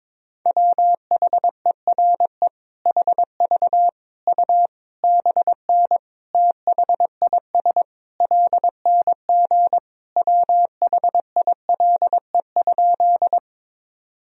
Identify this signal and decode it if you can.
Morse code, heard as a keyed tone, about 22 wpm, 700 Hz. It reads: WHERE HV U BN THIS LNG WHILE?
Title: Sentences from Top 300 Words and Abbreviations 22wpm